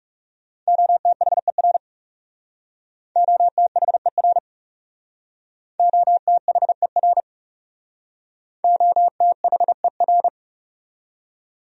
Morse code, heard as a keyed tone, about 36 wpm, 700 Hz. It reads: OTHER OTHER OTHER OTHER
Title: Speed-Racing Top 300 Words 30wpm